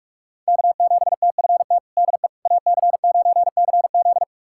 Morse code, heard as a keyed tone, about 46 wpm, 700 Hz. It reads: K7TFT DE AC9CZ
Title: USA Callsign DE Callsign - 5 Characters - 45wpm